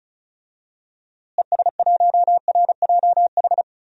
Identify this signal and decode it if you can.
Morse code, heard as a keyed tone, about 34 wpm, 700 Hz. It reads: ES1RJH
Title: DXCC-100 Call Signs - 6 Characters - Repeated 2x - 35wpm